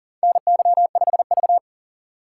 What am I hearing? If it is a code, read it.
Morse code, heard as a keyed tone, about 40 wpm, 700 Hz. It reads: NY5V
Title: USA Call Signs - 4 Characters - 3x Character Spacing - 40wpm